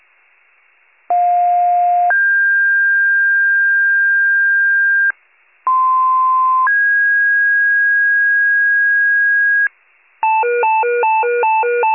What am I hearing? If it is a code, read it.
Two-tone page, heard as a one-second tone, then a three-second tone, about 710 then 1600 Hz, about 1040 then 1650 Hz. It is heard from 1.1 to 9.7 s.
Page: 95 RESCUE DUTY